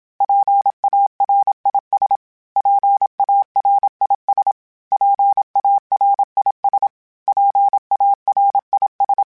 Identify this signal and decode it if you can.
Morse code, heard as a keyed tone, about 27 wpm, 800 Hz. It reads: PARIS PARIS PARIS PARIS
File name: PARIS@25